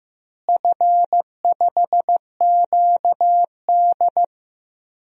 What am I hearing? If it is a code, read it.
Morse code, heard as a keyed tone, about 15 wpm, 700 Hz. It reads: F5QD
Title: DXCC-100 Call Signs - 4 Characters - Repeated 2x - 15wpm